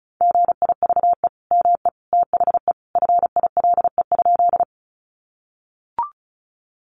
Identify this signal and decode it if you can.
Morse code, heard as a keyed tone, about 34 wpm, 700 Hz. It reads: GIVE ME THE FILE?
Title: Sentences from Top 1200 Words 35wpm